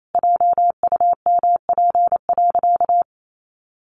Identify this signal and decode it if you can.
Morse code, heard as a keyed tone, about 28 wpm, 700 Hz. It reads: JUMP.
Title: Sentences from Top 1300 Words 28wpm